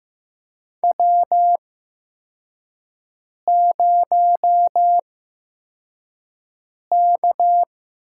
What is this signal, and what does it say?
Morse code, heard as a keyed tone, about 15 wpm, spaced as for 4 wpm, 700 Hz. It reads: W0K
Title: USA Call Signs - 4 Characters - 8x Character Spacing - 15wpm